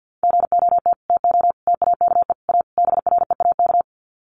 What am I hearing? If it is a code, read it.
Morse code, heard as a keyed tone, about 50 wpm, 700 Hz. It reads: GOT TO TAKE A BREAK
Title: Sets of 5 Words Encore 50wpm